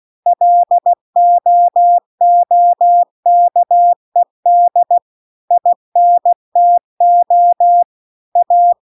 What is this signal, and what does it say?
Morse code, heard as a keyed tone, about 16 wpm, 700 Hz. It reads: LOOKED INTO A
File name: CW-The-Big-Twoheaded-River-1-16WPM